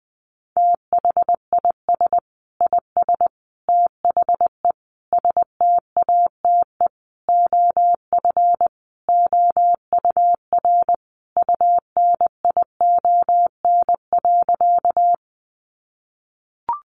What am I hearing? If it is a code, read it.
Morse code, heard as a keyed tone, about 20 wpm, 700 Hz. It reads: THIS IS THE STATE OF OUR UNION.
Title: Sentences from Top 1400 Words 20wpm